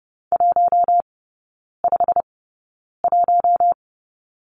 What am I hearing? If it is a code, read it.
Morse code, heard as a keyed tone, about 30 wpm, spaced as for 9 wpm, 700 Hz. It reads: 151
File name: Lesson 16e - Letter c - Three Characters 30wpm